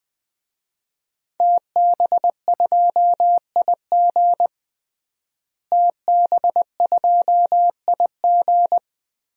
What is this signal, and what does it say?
Morse code, heard as a keyed tone, about 21 wpm, 700 Hz. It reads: TB2IG TB2IG
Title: DXCC-100 Call Signs - 5 Characters - Repeated 3x - 20wpm